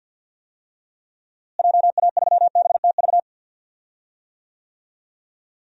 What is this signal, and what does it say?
Morse code, heard as a keyed tone, about 52 wpm, 700 Hz. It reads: JA3BTV
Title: DXCC-100 Call Signs - 6 Characters - Repeated 3x - 50wpm